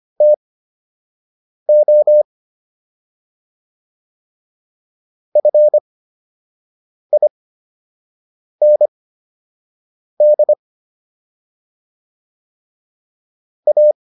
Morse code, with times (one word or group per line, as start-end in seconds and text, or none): 0.2-2.2 TO
5.4-10.5 FIND
13.7-13.9 A